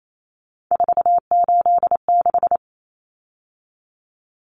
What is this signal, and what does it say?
Morse code, heard as a keyed tone, about 28 wpm, 700 Hz. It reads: 486
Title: Common Cut Numbers - 3-Digit 28wpm